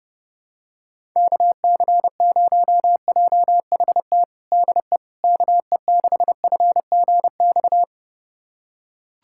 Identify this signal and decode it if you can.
Morse code, heard as a keyed tone, about 32 wpm, 700 Hz. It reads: KC0JHT DE KE6FGX
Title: USA Callsign DE Callsign - 6 Characters - 30wpm